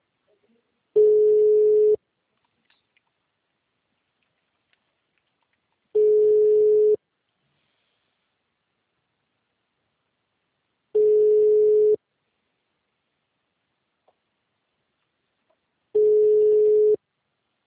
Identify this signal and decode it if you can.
Ringback tone (Europe, 425 Hz).